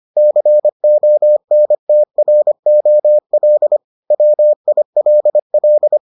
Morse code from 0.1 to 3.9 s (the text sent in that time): CONTROL